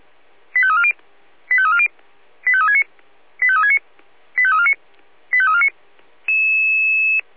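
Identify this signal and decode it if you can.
Five-tone selective call (ZVEI), heard as quick runs of five tones, tones of about 70 ms. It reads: 85340 85340 85369 85369 85349 85349
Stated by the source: ZVEI five-tone call received by SDR.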